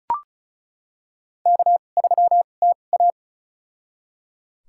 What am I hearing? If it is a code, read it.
Morse code, heard as a keyed tone, about 36 wpm, spaced as for 25 wpm, 700 Hz. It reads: K3TA
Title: USA Call Signs - 4 Characters - 2x Character Spacing - 35wpm